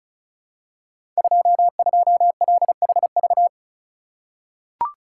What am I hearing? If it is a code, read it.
Morse code, heard as a keyed tone, about 36 wpm, 700 Hz. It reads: 22LHV
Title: Code Groups - 5 Letter:Numbers - 35wpm